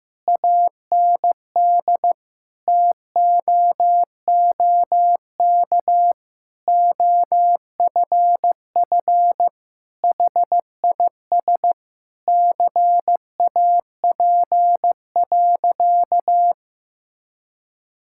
Morse code, heard as a keyed tone, about 16 wpm, 700 Hz. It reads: AND TOOK OFF HIS CAP.
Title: Sentences from Top 1600 Words 15wpm